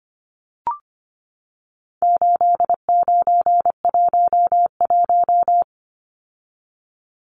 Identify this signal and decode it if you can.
Morse code, heard as a keyed tone, about 25 wpm, 700 Hz. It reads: 8911
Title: Numbers - 4-Digit 25wpm